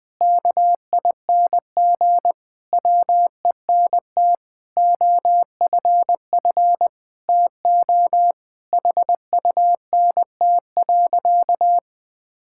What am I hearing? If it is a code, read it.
Morse code, heard as a keyed tone, about 20 wpm, 700 Hz. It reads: KING WENT OFF TO HUNT.
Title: Sentences from Top 1600 Words 20wpm